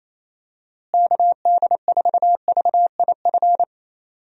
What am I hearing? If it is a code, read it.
Morse code, heard as a keyed tone, about 28 wpm, 700 Hz. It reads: KD4VIF
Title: USA Call Signs - 6 Characters - Repeated 3x - 28wpm